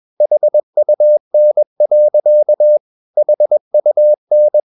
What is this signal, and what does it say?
Morse code, heard as a keyed tone, about 21 wpm, 600 Hz. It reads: HUN. HUN
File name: Ferie QRS 21wpm